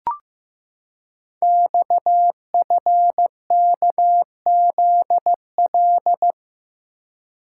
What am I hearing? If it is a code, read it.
Morse code, heard as a keyed tone, about 15 wpm, 700 Hz. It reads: XFKZL
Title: Code Groups - 5 Letters - 15wpm